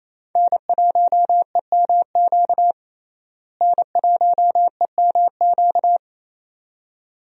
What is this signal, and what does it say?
Morse code, heard as a keyed tone, about 28 wpm, 700 Hz. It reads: N1EMQ N1EMQ
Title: USA Call Signs - 5 Characters - Repeated 2x - 28wpm